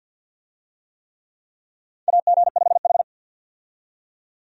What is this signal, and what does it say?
Morse code, heard as a keyed tone, about 48 wpm, 700 Hz. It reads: AG5H
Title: USA Call Signs - 4 Characters - 50wpm